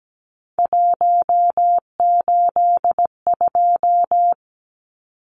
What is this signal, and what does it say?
Morse code, heard as a keyed tone, about 17 wpm, 700 Hz. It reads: 182
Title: Numbers - 3-Digit 17wpm